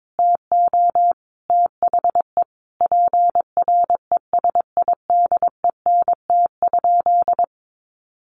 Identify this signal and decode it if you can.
Morse code, heard as a keyed tone, about 22 wpm, 700 Hz. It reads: TO THE PRESIDENT?
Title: Sentences from Top 1300 Words 22wpm